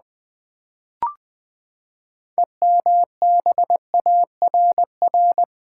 Morse code, heard as a keyed tone, about 20 wpm, 700 Hz. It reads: EMBARR
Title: Build A Word - 6 Letters At A Time - 12-Letter Words - 20wpm